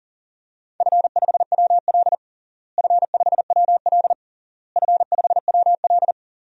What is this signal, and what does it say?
Morse code, heard as a keyed tone, about 40 wpm, 700 Hz. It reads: F5WL F5WL F5WL
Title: DXCC-100 Call Signs - 4 Characters - Repeated 3x - 40wpm